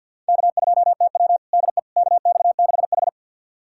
Morse code, heard as a keyed tone, about 50 wpm, 700 Hz. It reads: K2TW DE KX6H
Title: USA Callsign DE Callsign - 4 Characters - 50wpm